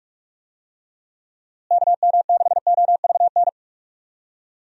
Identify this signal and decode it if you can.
Morse code, heard as a keyed tone, about 46 wpm, 700 Hz. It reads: KM6OVN
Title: USA Call Signs - 6 Characters - Repeated 2x - 45wpm